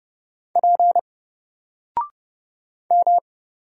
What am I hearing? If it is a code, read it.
Morse code, heard as a keyed tone, about 30 wpm, 700 Hz. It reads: P M
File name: Single-Letter-Number-30wpm